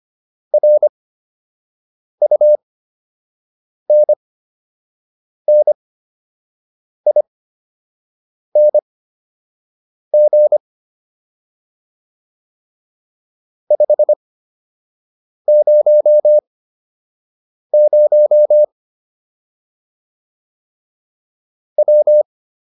Morse code, heard as a keyed tone, about 25 wpm, spaced as for 6 wpm, 600 Hz. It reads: RUNNING 500 W